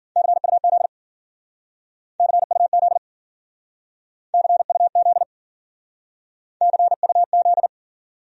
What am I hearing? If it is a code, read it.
Morse code, heard as a keyed tone, about 49 wpm, 700 Hz. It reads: CUZ CUZ CUZ CUZ
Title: Speed-Racing Abbreviations - 40wpm